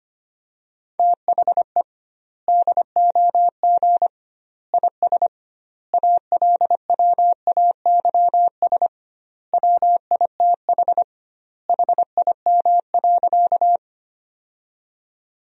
Morse code, heard as a keyed tone, about 25 wpm, 700 Hz. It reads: THE DOG IS ALWAYS WITH HIM.
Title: 2x Word-Spacing - Sentences from Top 400 Words - 25wpm